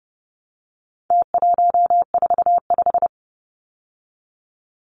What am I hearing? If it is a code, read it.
Morse code, heard as a keyed tone, about 30 wpm, 700 Hz. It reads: T145
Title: Numbers - Contest Style - 4-Digit 30wpm